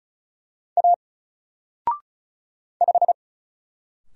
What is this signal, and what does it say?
Morse code, heard as a keyed tone, about 33 wpm, 700 Hz. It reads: A 5
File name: Single-Letter-Number-35wpm